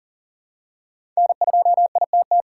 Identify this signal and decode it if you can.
Morse code, heard as a keyed tone, about 40 wpm, 700 Hz. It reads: N2ITT
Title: USA Call Signs - 5 Characters - 6x Character Spacing - 40wpm